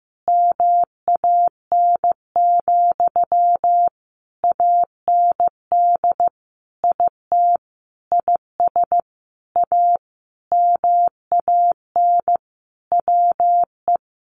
Morse code, heard as a keyed tone, about 15 wpm, 700 Hz. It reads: MAN, AND IT IS A MAN WE
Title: Sentences from Top 200 Words 15wpm